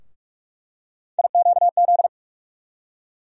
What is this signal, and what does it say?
Morse code, heard as a keyed tone, about 46 wpm, 700 Hz. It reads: IQZ
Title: USA Call Sign Postfixes - 3 Letters 45wpm